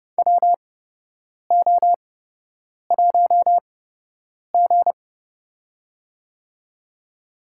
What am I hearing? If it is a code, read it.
Morse code, heard as a keyed tone, about 30 wpm, spaced as for 8 wpm, 700 Hz. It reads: WO1G
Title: USA Call Signs - 4 Characters - 8x Character Spacing - 30wpm